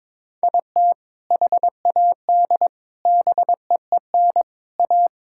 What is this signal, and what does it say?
Morse code, heard as a keyed tone, about 23 wpm, 700 Hz. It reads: IT HAD BEEN A
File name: practice / 22wpm / Sentences with 2-Word Phrases - 22wpm